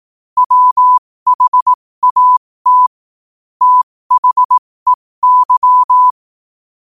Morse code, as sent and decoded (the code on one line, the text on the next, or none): .-- .... .- - / - .... -.--
WHAT THEY